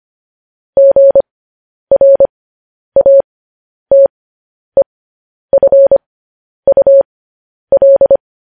Morse code, heard as a keyed tone, about 25 wpm, spaced as for 10 wpm, 550 Hz. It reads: GRATEFUL
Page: news headlines in morse code at 10 wpm